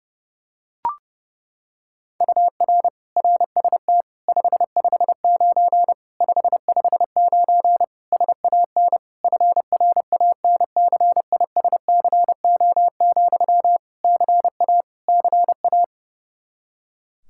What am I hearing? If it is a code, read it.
Morse code, heard as a keyed tone, about 30 wpm, 700 Hz. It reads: UR RST 559 559 SAN FRANCISCO, CA CA
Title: Practice QSO - Exchange 1 - Beginning - 30wpm